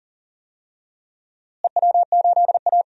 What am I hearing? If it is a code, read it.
Morse code, heard as a keyed tone, about 39 wpm, 700 Hz. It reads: EW8A
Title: DXCC-100 Call Signs - 4 Characters - Repeated 2x - 40wpm